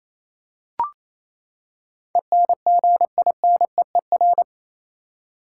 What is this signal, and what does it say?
Morse code, heard as a keyed tone, about 28 wpm, 700 Hz. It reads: ENGINEER
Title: Practice QSO - Exchange 3 - More Info - 28wpm